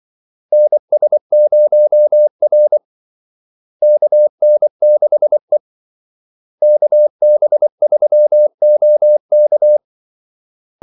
Morse code, heard as a keyed tone, about 24 wpm, 600 Hz. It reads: NS0R KN6E KB3OK
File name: Callsigns-24wpm